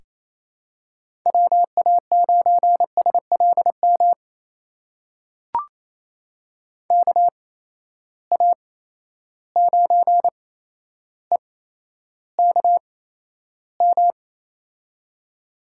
Morse code, heard as a keyed tone, about 28 wpm, spaced as for 12 wpm, 700 Hz. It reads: WA9SLM KA9EKM
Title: USA Call Signs - 6 Characters - 8x Character Spacing - 28wpm